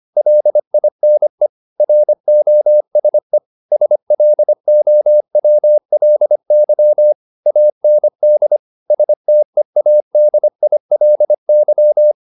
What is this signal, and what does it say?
Morse code, heard as a keyed tone, about 25 wpm, 600 Hz. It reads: LINE ROSE SLOWLY AND STEADILY
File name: the old man and the sea part 1 25wpm